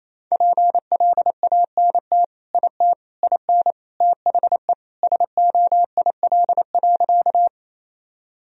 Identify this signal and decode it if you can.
Morse code, heard as a keyed tone, about 28 wpm, 700 Hz. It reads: PLANT IT IN THE SOIL.
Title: Sentences from Top 1000 Words - Review Part 4 - 28wpm